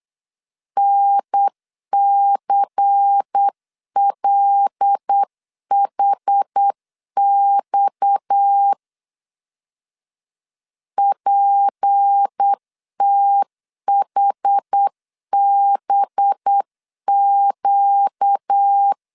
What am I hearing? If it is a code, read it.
Morse code, heard as a keyed tone, about 8 wpm, 800 Hz. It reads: NCLHX PTHBQ
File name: CW / DET / 1 6 1500C S C 8wpm 5219